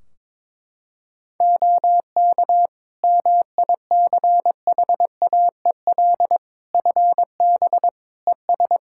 Morse code, heard as a keyed tone, about 22 wpm, 700 Hz. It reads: OK MICHAEL FB ES
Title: K1USN SST QSO Practice 22wpm